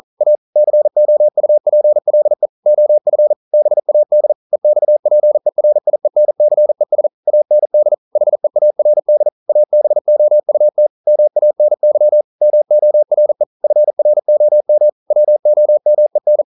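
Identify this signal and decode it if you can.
Morse code, heard as a keyed tone, about 42 wpm, 600 Hz. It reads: A COUPLE OF BAD EXPERIENCES AND HEARD ABOUT MANY MORE FROM WOMEN
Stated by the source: Morse code practice at 41 wpm, from CW med Gnister.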